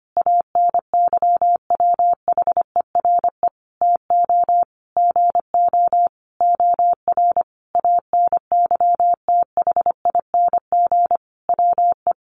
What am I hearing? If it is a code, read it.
Morse code, heard as a keyed tone, about 25 wpm, 700 Hz. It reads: ANYWHERE TO GO OR ANYTHING WE
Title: Sentences from Top 1400 Words 25wpm